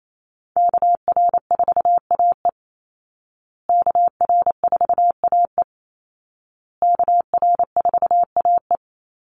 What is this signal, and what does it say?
Morse code, heard as a keyed tone, about 28 wpm, 700 Hz. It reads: KR4AE KR4AE KR4AE
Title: Call Signs Repeated 2x - 28wpm